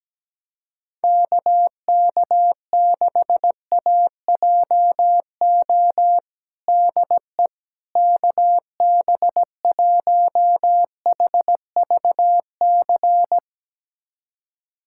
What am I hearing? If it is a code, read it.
Morse code, heard as a keyed tone, about 17 wpm, 700 Hz. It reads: KK6AJO DE KB1HVC
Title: USA Callsign DE Callsign 17wpm